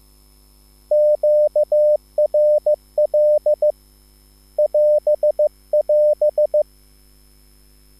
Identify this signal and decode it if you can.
Morse code, heard as a keyed tone, about 15 wpm, 600 Hz. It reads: QRL <AS><AS>